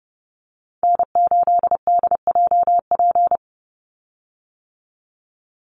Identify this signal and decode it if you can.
Morse code, heard as a keyed tone, about 30 wpm, 700 Hz. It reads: N8DJP
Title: USA Call Signs 30wpm